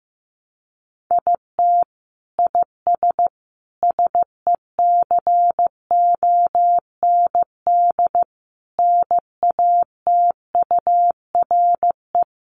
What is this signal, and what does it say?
Morse code, heard as a keyed tone, about 15 wpm, 700 Hz. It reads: IT IS SECOND NATURE
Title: Sentences from Top 600 Words 15wpm